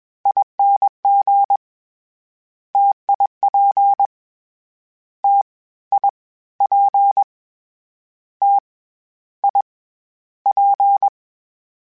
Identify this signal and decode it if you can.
Morse code, heard as a keyed tone, about 21 wpm, 800 Hz. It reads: ING TIP T I P T I P
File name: Difficult Combos / Difficult Combos-mp3s / T AOEI @20